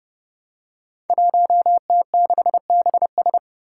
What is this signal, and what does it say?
Morse code, heard as a keyed tone, about 30 wpm, 700 Hz. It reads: 1T6BS
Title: Five Challenging Letter-Number Combinations 30wpm